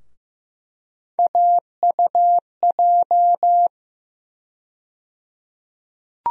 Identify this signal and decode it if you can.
Morse code, heard as a keyed tone, about 15 wpm, 700 Hz. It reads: AUJ
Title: USA Call Sign Postfixes - 3 Letters 15wpm